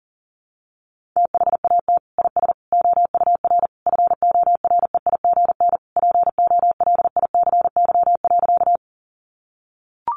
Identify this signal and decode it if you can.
Morse code, heard as a keyed tone, about 37 wpm, 700 Hz. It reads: THAT IS OUR FOREIGN POLICY.
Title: Sentences from Top 1400 Words 40wpm